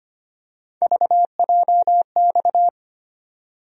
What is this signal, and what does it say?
Morse code, heard as a keyed tone, about 25 wpm, 700 Hz. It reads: VJX
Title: USA Call Sign Postfixes - 3 Letters 25wpm